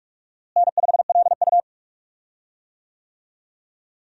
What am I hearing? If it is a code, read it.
Morse code, heard as a keyed tone, about 46 wpm, 700 Hz. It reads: N5LU
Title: USA Call Signs - 4 Characters - Repeated 3x - 45wpm